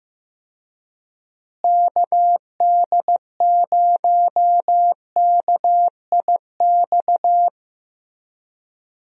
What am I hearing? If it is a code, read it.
Morse code, heard as a keyed tone, about 15 wpm, 700 Hz. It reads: KD0KIX
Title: USA Call Signs - 6 Characters - Repeated 3x - 15wpm